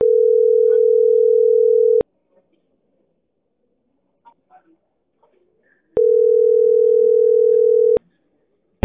Ringback tone (North America), 440 + 480 Hz.